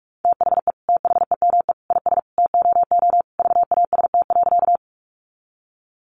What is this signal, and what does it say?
Morse code, heard as a keyed tone, about 44 wpm, 700 Hz. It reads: THE THEME IS TOO VAST.
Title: Sentences from Top 1300 Words 45wpm